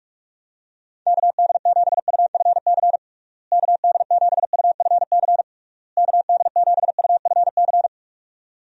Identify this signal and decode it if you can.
Morse code, heard as a keyed tone, about 42 wpm, 700 Hz. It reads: KD7UFC KD7UFC KD7UFC
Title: USA Call Signs - 6 Characters - Repeated 3x - 45wpm